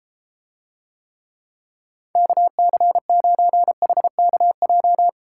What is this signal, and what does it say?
Morse code, heard as a keyed tone, about 33 wpm, 700 Hz. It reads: KC9HKJ